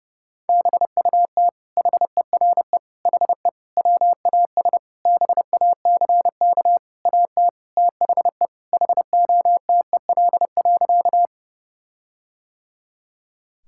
Morse code, heard as a keyed tone, about 30 wpm, 700 Hz. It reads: BUT HERE HE WAS BACK AT THE HOTEL.
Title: Sentences from Top 1600 Words 30wpm